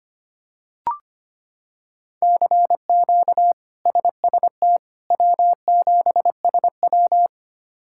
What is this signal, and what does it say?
Morse code, heard as a keyed tone, about 25 wpm, 700 Hz. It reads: CQ SST W7SW
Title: K1USN SST QSO Practice - 25wpm